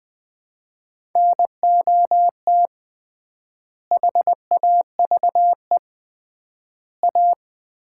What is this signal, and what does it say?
Morse code, heard as a keyed tone, about 21 wpm, 700 Hz. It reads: NOT HAVE A
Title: 3x Word-Spacing - Sets of 5 Words - 20wpm